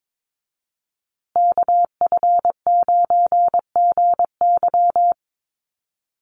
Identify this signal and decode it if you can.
Morse code, heard as a keyed tone, about 22 wpm, 700 Hz. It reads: KF9GY
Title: USA Call Signs Repeated 2x 22wpm